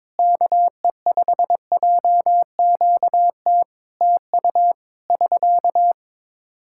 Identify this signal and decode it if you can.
Morse code, heard as a keyed tone, about 22 wpm, 700 Hz. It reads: KE5JQT TU <SK>
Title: Practice QSO - Exchange 4 - Ending - 22wpm